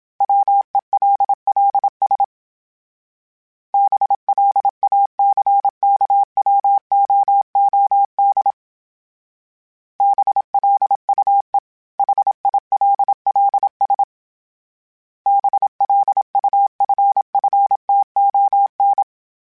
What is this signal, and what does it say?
Morse code, heard as a keyed tone, about 27 wpm, 800 Hz. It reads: WELLS BLACKWOOD BLUE HILLS BLUFFTON
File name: fromSMs / FromSMs mp3s / Cities@25